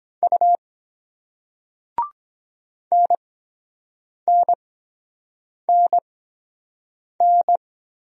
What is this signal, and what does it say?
Morse code, heard as a keyed tone, about 23 wpm, spaced as for 7 wpm, 700 Hz. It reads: U NNNN